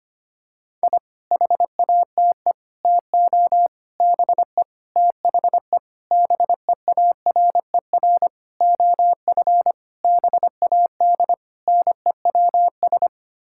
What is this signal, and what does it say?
Morse code, heard as a keyed tone, about 25 wpm, 700 Hz. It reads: I HATE TO BE THE BEARER OF BAD NEWS
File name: Sets of 9 Words - 25wpm